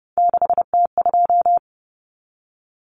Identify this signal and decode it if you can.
Morse code, heard as a keyed tone, about 30 wpm, 700 Hz. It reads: 6T2
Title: Common Cut Numbers - 3-Digit 30wpm